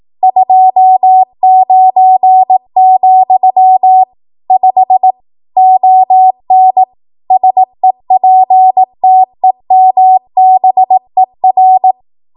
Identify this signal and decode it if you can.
Morse code, heard as a keyed tone, about 18 wpm, 750 Hz. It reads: 29, 5 ON SEPTEMBER
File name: Archive / 18 WPM / 18WPM